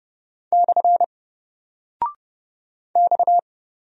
Morse code, heard as a keyed tone, about 30 wpm, 700 Hz. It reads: / X
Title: Single Character (Letter:Number:Punctuation:Prosign) 30wpm